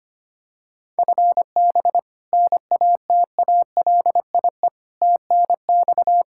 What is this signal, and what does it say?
Morse code, heard as a keyed tone, about 25 wpm, 700 Hz. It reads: FB NATALIE TNX
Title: Practice QSO - Exchange 3 - More Info - 25wpm